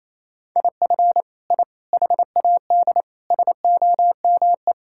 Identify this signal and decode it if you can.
Morse code, heard as a keyed tone, about 28 wpm, 700 Hz. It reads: IF I HAD SOME
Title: Sentences from Top 1000 Words - Review Part 5 - 28wpm